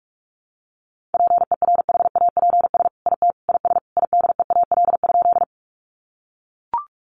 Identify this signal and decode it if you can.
Morse code, heard as a keyed tone, about 44 wpm, 700 Hz. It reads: PERHAPS IT IS IDEAL?
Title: Sentences from Top 1400 Words 45wpm